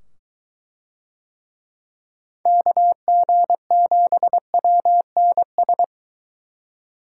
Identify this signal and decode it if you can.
Morse code, heard as a keyed tone, about 23 wpm, 700 Hz. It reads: KG7WNS